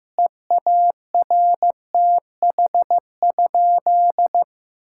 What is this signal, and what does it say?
Morse code, heard as a keyed tone, about 15 wpm, 700 Hz. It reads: EARTH?
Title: Sentences from Top 1600 Words 15wpm